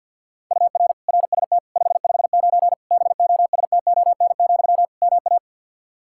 Morse code, heard as a keyed tone, about 52 wpm, 700 Hz. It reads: UR RST 559 BOSTON, MA
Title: Practice QSO - Exchange 1 - Beginning - 50wpm